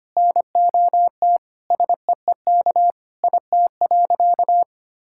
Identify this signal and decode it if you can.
Morse code, heard as a keyed tone, about 25 wpm, 700 Hz. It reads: NOT SEEK IT.
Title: Sentences from Top 1600 Words 25wpm